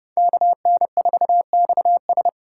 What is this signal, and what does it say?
Morse code, heard as a keyed tone, about 30 wpm, 700 Hz. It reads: KN4XS
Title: USA Call Signs - 3x Character Spacing - 30wpm